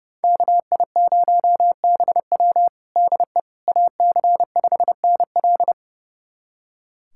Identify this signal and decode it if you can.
Morse code, heard as a keyed tone, about 32 wpm, 700 Hz. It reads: KI0BW DE AC5NL
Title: USA Callsign DE Callsign - 5 Characters - 30wpm